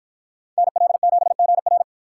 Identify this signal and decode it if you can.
Morse code, heard as a keyed tone, about 53 wpm, 700 Hz. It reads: NL7GR